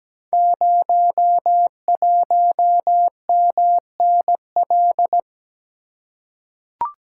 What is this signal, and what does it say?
Morse code, heard as a keyed tone, about 17 wpm, 700 Hz. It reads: 01MNL
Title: Code Groups - 5 Letter:Numbers - 17wpm